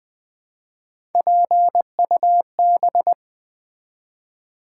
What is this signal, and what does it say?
Morse code, heard as a keyed tone, about 20 wpm, 700 Hz. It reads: PUB
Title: Build A Word - 1 Letter At A Time - 3-Letter Words - 20wpm